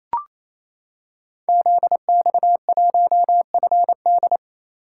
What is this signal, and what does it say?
Morse code, heard as a keyed tone, about 27 wpm, 700 Hz. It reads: ZX1FD
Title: Code Groups - 5 Letter:Numbers - 28wpm